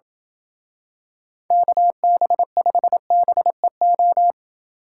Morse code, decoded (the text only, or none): KB5BEO